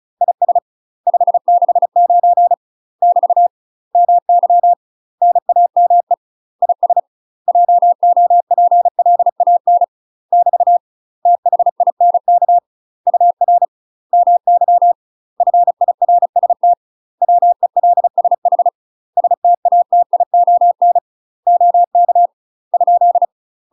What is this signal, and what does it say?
Morse code, heard as a keyed tone, about 35 wpm, 700 Hz. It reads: IS 569 <BT> MY NAME IS JOPLAN <BT> THINK UR MY FIRST WELSH STATION OK ?
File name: UK MASTER QSOs TRAINING / 35WPM / UK MASTER QSOs 0029